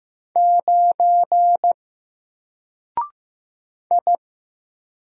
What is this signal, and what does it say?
Morse code, heard as a keyed tone, about 15 wpm, 700 Hz. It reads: 9 I
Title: Single Letter-Number Alt 15wpm